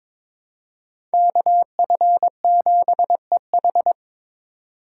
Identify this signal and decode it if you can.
Morse code, heard as a keyed tone, about 26 wpm, 700 Hz. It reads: KF7EH
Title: Speed Racing - USA 2x2 Call Signs 22wpm